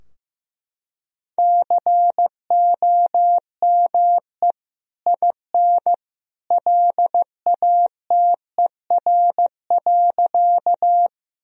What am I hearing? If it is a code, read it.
Morse code, heard as a keyed tone, about 15 wpm, 700 Hz. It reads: COME IN LATER.
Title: Sentences from Top 1000 Words - Review Part 4 - 15wpm